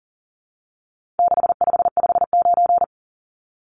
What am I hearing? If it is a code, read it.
Morse code, heard as a keyed tone, about 40 wpm, 700 Hz. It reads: 6559
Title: Numbers - 4-Digit 40wpm